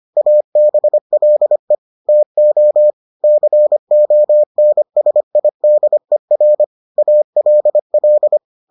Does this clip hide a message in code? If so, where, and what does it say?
Yes: Morse code from 0.2 to 8.4 s, heard as a keyed tone, about 25 wpm, 600 Hz. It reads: ABLE TO CONSIDER ALL